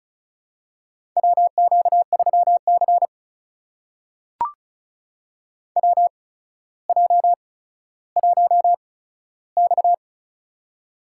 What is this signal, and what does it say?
Morse code, heard as a keyed tone, about 36 wpm, spaced as for 15 wpm, 700 Hz. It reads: WQ3C WJ1X
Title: USA Call Signs - 4 Characters - 8x Character Spacing - 35wpm